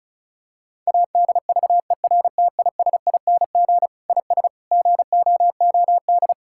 Morse code, heard as a keyed tone, about 34 wpm, 700 Hz. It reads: ADVERTISING IS GOOD
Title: Sentences from Top 1000 Words - Review Part 6 - 35wpm